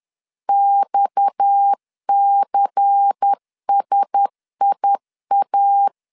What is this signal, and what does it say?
Morse code, heard as a keyed tone, about 10 wpm, 800 Hz. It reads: XCSIA
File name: LEZIONI / 6 PKX0 / 6 PKX 10wpm 2426